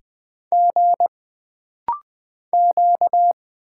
Morse code, heard as a keyed tone, about 20 wpm, 700 Hz. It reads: G Q
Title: Single Letters - Rapid-Fire 20wpm